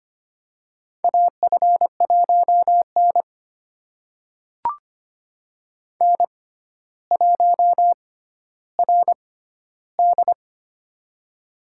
Morse code, heard as a keyed tone, about 25 wpm, spaced as for 13 wpm, 700 Hz. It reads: AF1N N1RD
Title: USA Call Signs - 4 Characters - 6x Character Spacing - 25wpm